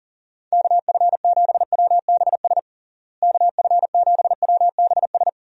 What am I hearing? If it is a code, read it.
Morse code, heard as a keyed tone, about 40 wpm, 700 Hz. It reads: KF7WBS KF7WBS
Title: USA Call Signs - 6 Characters - Repeated 3x - 40wpm